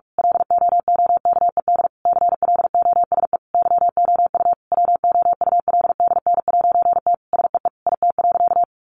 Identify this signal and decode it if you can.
Morse code, heard as a keyed tone, about 44 wpm, 700 Hz. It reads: LOOKED CLOSE YOU WOULDN'T SEE IT.
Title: A Sound Of Thunder 45wpm